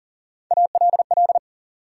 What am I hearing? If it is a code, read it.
Morse code, heard as a keyed tone, about 40 wpm, 700 Hz. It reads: ALL
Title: Sentences from Top 100 Words - New Word Drill - 40wpm